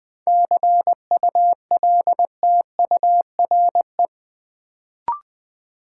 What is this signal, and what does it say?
Morse code, heard as a keyed tone, about 20 wpm, 700 Hz. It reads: CULTURE
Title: Sentences from Top 1000 Words - Review Part 6 - New Word Drill - 20wpm